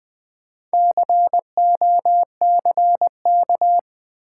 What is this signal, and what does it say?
Morse code, heard as a keyed tone, about 20 wpm, 700 Hz. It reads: COCK
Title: Build A Word - 2 Letters At A Time - 8-Letter Words - 20wpm